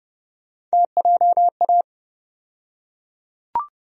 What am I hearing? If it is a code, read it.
Morse code, heard as a keyed tone, about 31 wpm, 700 Hz. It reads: TJA
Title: USA Call Sign Postfixes - 3 Letters 30wpm